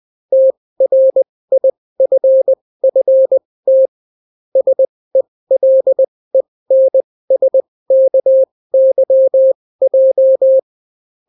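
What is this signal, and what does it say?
Morse code, heard as a keyed tone, about 20 wpm, spaced as for 16 wpm, 530 Hz. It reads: TRIFFT SELENSKYJ
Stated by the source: Daily Morse weather and top stories at 16 wpm, clean without QRN.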